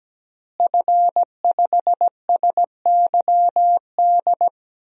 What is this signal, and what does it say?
Morse code, heard as a keyed tone, about 17 wpm, 700 Hz. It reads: F5SYD
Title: DXCC-100 Call Signs - 5 Characters - Repeated 2x - 17wpm